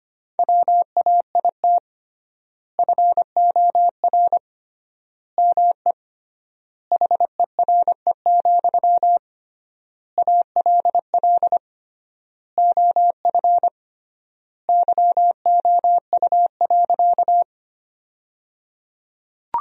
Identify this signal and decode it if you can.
Morse code, heard as a keyed tone, about 25 wpm, 700 Hz. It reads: WAIT FOR ME HERE, ALL OF YOU.
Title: 3x Word-Spacing - Sentences from Top 500 Words - 25wpm